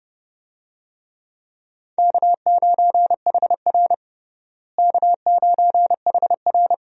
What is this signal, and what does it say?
Morse code, heard as a keyed tone, about 32 wpm, 700 Hz. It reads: K9HR K9HR
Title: USA Call Signs - 4 Characters - Repeated 3x - 30wpm